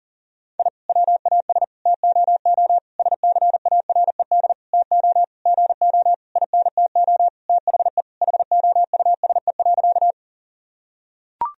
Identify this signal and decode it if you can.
Morse code, heard as a keyed tone, about 40 wpm, 700 Hz. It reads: I WAS TOO SCARED TO GO INTO THE HOUSE.
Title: Sentences from Top 1000 Words - Review Part 4 - 40wpm